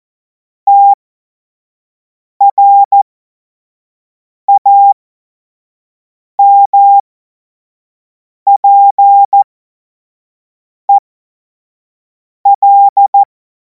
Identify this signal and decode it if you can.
Morse code, heard as a keyed tone, about 14 wpm, spaced as for 5 wpm, 800 Hz. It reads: TRAMPEL
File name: Morsekurs / Witze / Witz 16. Juni 2026 25 F70